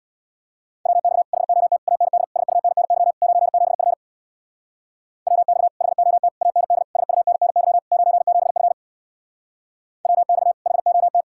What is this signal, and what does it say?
Morse code, heard as a keyed tone, about 67 wpm, 700 Hz. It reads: WX HOT AND SUNNY 97F WX HOT AND SUNNY 97F WX HOT